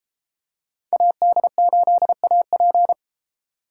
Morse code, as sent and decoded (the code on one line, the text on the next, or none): .- -.. ---.. .- .--.
AD8AP